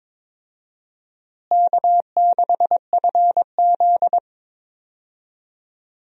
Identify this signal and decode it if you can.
Morse code, heard as a keyed tone, about 22 wpm, 700 Hz. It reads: K6FZ
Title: USA Call Signs - 4 Characters - Repeated 2x - 22wpm